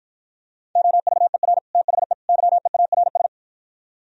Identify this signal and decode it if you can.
Morse code, heard as a keyed tone, about 52 wpm, 700 Hz. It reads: OVER THE YEARS